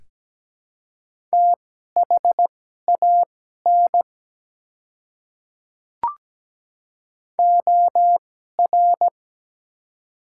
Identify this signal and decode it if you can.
Morse code, heard as a keyed tone, about 17 wpm, spaced as for 12 wpm, 700 Hz. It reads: THAN OR
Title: 2x Character Spacing - Top 300 Words - 17wpm